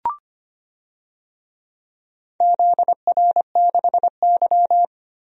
Morse code, heard as a keyed tone, about 25 wpm, 700 Hz. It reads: ZR6Y
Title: DXCC-100 Call Signs - 4 Characters - Repeated 3x - 25wpm